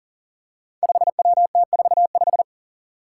Morse code, heard as a keyed tone, about 40 wpm, 700 Hz. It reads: 5WT45
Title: Five Challenging Letter-Number Combinations 40wpm